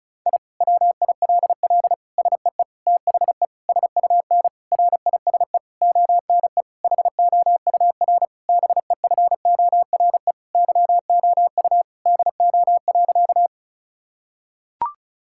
Morse code, heard as a keyed tone, about 36 wpm, 700 Hz. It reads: I WILL SEE THE SUN RISE ONE HOUR BEFORE YOU DO.
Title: Sentences from Top 1000 Words - Review Part 6 - 35wpm